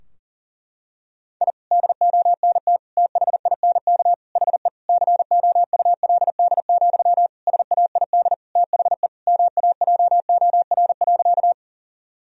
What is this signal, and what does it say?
Morse code, heard as a keyed tone, about 40 wpm, 700 Hz. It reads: I DONT THINK HE COULD, SAID THE MAJOR.
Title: Sentences from Top 1000 Words - Review Part 4 - 40wpm